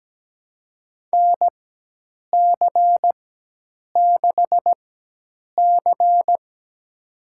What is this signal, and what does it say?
Morse code, heard as a keyed tone, about 17 wpm, spaced as for 8 wpm, 700 Hz. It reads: NC6C
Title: USA Call Signs - 4 Characters - 4x Character Spacing - 17wpm